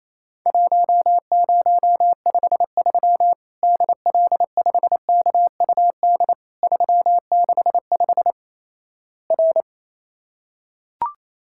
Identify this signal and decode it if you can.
Morse code, heard as a keyed tone, about 28 wpm, 700 Hz. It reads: 1053 DL5KUD 365 R
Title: Worked All Europe DX Contest - QTC Exchange - 28wpm